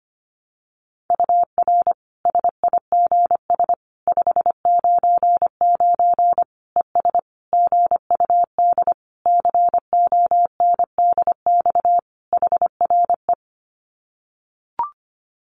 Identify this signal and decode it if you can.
Morse code, heard as a keyed tone, about 25 wpm, 700 Hz. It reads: UR SIGS 599 ES GUD CONDX HRE
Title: Common QSO Phrases 25wpm